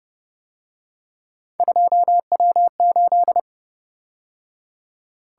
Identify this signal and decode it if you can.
Morse code, heard as a keyed tone, about 29 wpm, 700 Hz. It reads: 2W8